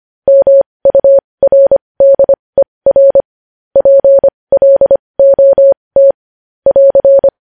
Morse code, heard as a keyed tone, about 25 wpm, spaced as for 20 wpm, 550 Hz. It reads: MURDER PLOT <AR>
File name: podcast / News headlines-1x-25x20